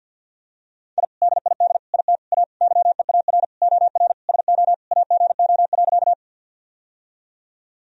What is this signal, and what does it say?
Morse code, heard as a keyed tone, about 46 wpm, 700 Hz. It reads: I DID IT A YEAR OR SO AGO.
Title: Sentences from Top 1000 Words - Review Part 3 - 50wpm